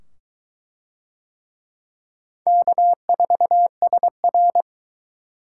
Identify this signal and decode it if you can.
Morse code, heard as a keyed tone, about 22 wpm, 700 Hz. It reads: K4SR